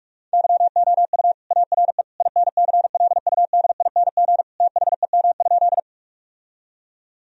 Morse code, heard as a keyed tone, about 41 wpm, 700 Hz. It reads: YOU ARE INCLUDING THEM?
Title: Sentences from Top 1000 Words - Review Part 4 - 45wpm